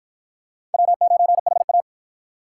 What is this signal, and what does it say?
Morse code, heard as a keyed tone, about 48 wpm, 700 Hz. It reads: W9HA